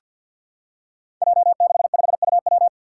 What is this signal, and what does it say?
Morse code, heard as a keyed tone, about 50 wpm, 700 Hz. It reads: J65UW
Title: Five Challenging Letter-Number Combinations 50wpm